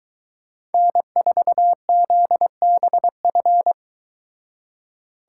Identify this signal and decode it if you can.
Morse code, heard as a keyed tone, about 23 wpm, 700 Hz. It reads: N4ZBF